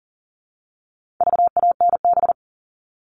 Morse code, heard as a keyed tone, about 40 wpm, 700 Hz. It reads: VANB
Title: All Cut Numbers - 4-Digit 40wpm